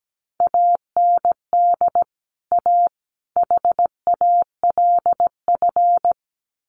Morse code, heard as a keyed tone, about 17 wpm, 700 Hz. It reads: AND A HALF
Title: Sets of 3 Words 17wpm